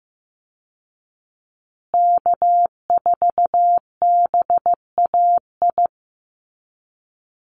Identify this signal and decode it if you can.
Morse code, heard as a keyed tone, about 15 wpm, 700 Hz. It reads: K4BAI
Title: Call Signs Repeated 3x - 15wpm